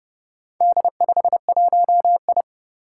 Morse code, heard as a keyed tone, about 30 wpm, 700 Hz. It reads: D51I
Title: Four Challenging Letter-Number Combinations 30wpm